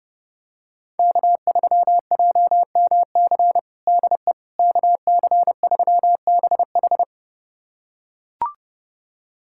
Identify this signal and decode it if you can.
Morse code, heard as a keyed tone, about 32 wpm, 700 Hz. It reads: K3JMC DE KC3BH
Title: USA Callsign DE Callsign - 5 Characters - 30wpm